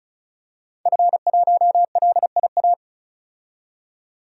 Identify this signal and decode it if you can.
Morse code, heard as a keyed tone, about 36 wpm, 700 Hz. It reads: F1LIA
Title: DXCC-100 Call Signs - 5 Characters - 35wpm